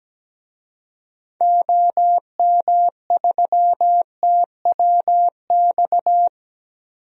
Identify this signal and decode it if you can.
Morse code, heard as a keyed tone, about 17 wpm, 700 Hz. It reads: OM3TWX
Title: DXCC-100 Call Signs - 6 Characters - 17wpm